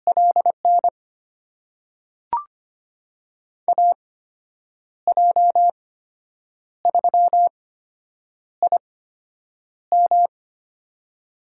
Morse code, heard as a keyed tone, about 25 wpm, spaced as for 8 wpm, 700 Hz. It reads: LN AJ3IM
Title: USA Call Signs - 5 Characters - 8x Character Spacing - 25wpm